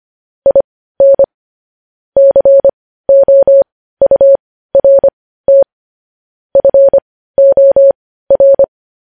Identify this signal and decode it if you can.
Morse code, heard as a keyed tone, about 25 wpm, spaced as for 15 wpm, 550 Hz. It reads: IN COURT FOR
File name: podcast / News headlines-1x-25x15